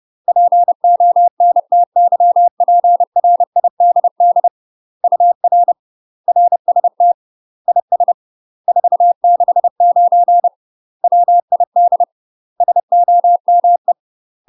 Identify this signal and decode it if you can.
Morse code, heard as a keyed tone, about 30 wpm, 700 Hz. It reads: PONTYPRIDD UR RST IS 469 WID SOME
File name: UK MASTER QSOs TRAINING / 30WPM / UK MASTER QSOs 0042